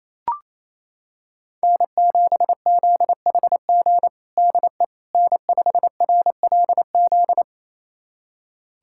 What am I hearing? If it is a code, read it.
Morse code, heard as a keyed tone, about 28 wpm, 700 Hz. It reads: N7ZHG DE N5RLZ
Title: USA Callsign DE Callsign - 5 Characters - 28wpm